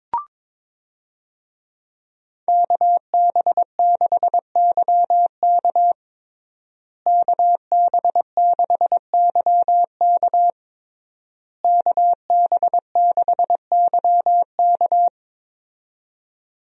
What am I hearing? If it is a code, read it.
Morse code, heard as a keyed tone, about 22 wpm, 700 Hz. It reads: KB6YK KB6YK KB6YK
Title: USA Call Signs - 5 Characters - Repeated 3x - 22wpm